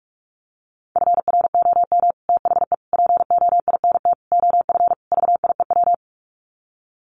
Morse code, heard as a keyed tone, about 44 wpm, 700 Hz. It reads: FROM THE POINT OF VIEW
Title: Sets of 5 Words 45wpm